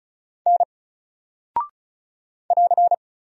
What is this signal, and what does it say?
Morse code, heard as a keyed tone, about 36 wpm, 700 Hz. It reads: N <AR>
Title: Single Character (Letter:Number:Punctuation:Prosign) Alt 35wpm